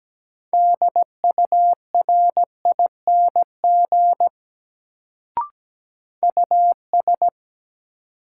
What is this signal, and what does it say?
Morse code, heard as a keyed tone, about 17 wpm, 700 Hz. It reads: DURING US
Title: Sets of 2 Words - New Word Drill - 17wpm